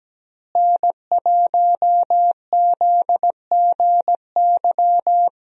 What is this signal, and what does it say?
Morse code, heard as a keyed tone, about 17 wpm, 700 Hz. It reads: N1ZGY
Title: USA Call Signs - 5 Characters - Repeated 2x - 17wpm